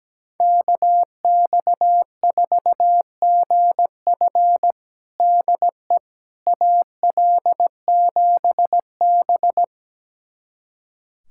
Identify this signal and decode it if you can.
Morse code, heard as a keyed tone, about 17 wpm, 700 Hz. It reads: KX4GF DE AL7B
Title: USA Callsign DE Callsign 17wpm